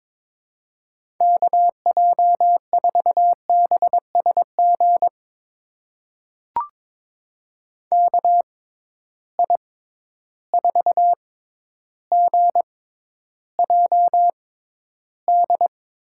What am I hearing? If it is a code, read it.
Morse code, heard as a keyed tone, about 22 wpm, spaced as for 11 wpm, 700 Hz. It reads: KJ4BSG KI4GJD
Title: USA Call Signs - 6 Characters - 6x Character Spacing - 22wpm